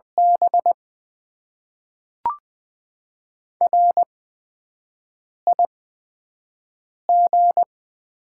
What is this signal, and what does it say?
Morse code, heard as a keyed tone, about 20 wpm, 700 Hz. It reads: B R I G
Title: Top 20 QSO Elements - 8x Character Spacing - 20wpm